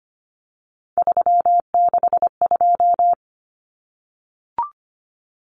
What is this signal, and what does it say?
Morse code, heard as a keyed tone, about 25 wpm, 700 Hz. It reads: 362
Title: Common Cut Numbers - 3-Digit 25wpm